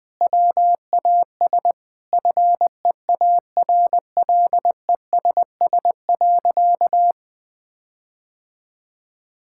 Morse code, heard as a keyed tone, about 20 wpm, 700 Hz. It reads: WAS FEARLESS.
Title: Sentences with 4-Word Phrases - 20wpm